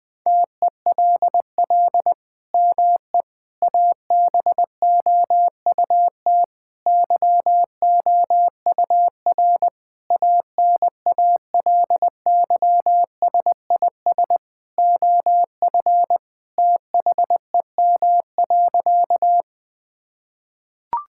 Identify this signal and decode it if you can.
Morse code, heard as a keyed tone, about 21 wpm, 700 Hz. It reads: TELL ME ABOUT YOUR ANALYSIS OF THEM.
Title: Sentences from Top 1000 Words - Review Part 5 - 20wpm